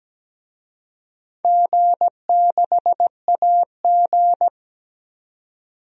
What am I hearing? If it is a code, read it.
Morse code, heard as a keyed tone, about 17 wpm, 700 Hz. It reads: G6AG
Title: DXCC-100 Call Signs - 4 Characters - 17wpm